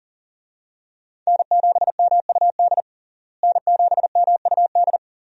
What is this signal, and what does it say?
Morse code, heard as a keyed tone, about 40 wpm, 700 Hz. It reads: N7MUD N7MUD
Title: USA Call Signs - 5 Characters - Repeated 2x - 40wpm